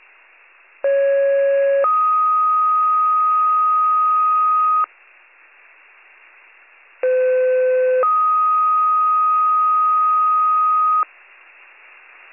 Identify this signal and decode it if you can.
Two-tone page, heard as a one-second tone, then a three-second tone, about 550 then 1250 Hz, about 510 then 1250 Hz. It is heard from 0.8 to 11.0 s.